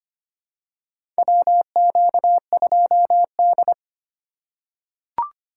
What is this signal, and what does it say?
Morse code, heard as a keyed tone, about 25 wpm, 700 Hz. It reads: WQ2D
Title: USA Call Signs - 4 Characters - 4x Character Spacing - 25wpm